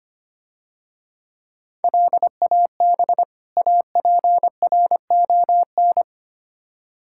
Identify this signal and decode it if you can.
Morse code, heard as a keyed tone, about 27 wpm, 700 Hz. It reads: LAB APRON
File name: practice / 25wpm / Difficult Words - Speed-Racing with Hints - 25wpm